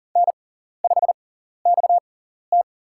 Morse code, heard as a keyed tone, about 37 wpm, spaced as for 13 wpm, 700 Hz. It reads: N5XT
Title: USA Call Signs - 4 Characters - 6x Character Spacing - 40wpm